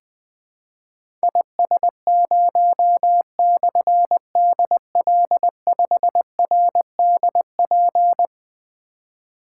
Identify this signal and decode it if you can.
Morse code, heard as a keyed tone, about 20 wpm, 700 Hz. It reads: IS0/DL5RDP
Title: Difficult Call Signs 20wpm